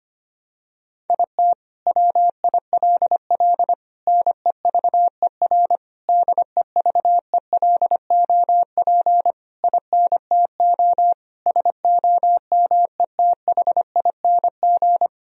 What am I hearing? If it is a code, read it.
Morse code, heard as a keyed tone, about 25 wpm, 700 Hz. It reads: IT WILL NEVER DEVELOP INTO SOMETHING
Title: Sentences from Top 1000 Words - Review Part 2 - 25wpm